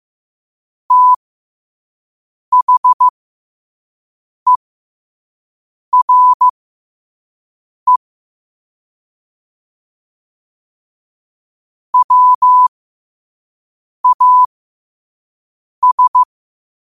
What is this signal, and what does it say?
Morse code, heard as a keyed tone, about 15 wpm, spaced as for 5 wpm, 1000 Hz. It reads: THERE WAS
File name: qotdfiles / 5wpm